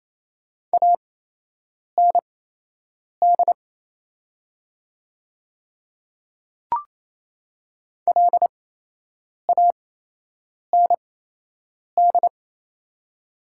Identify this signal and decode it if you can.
Morse code, heard as a keyed tone, about 28 wpm, spaced as for 8 wpm, 700 Hz. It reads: AND LAND
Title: 8x Character Spacing - Top 500 Words - 28wpm